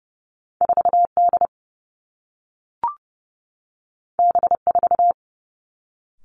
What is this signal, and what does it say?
Morse code, heard as a keyed tone, about 30 wpm, 700 Hz. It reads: 4 D B 4
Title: Two Challenging Letter-Number Combinations 30wpm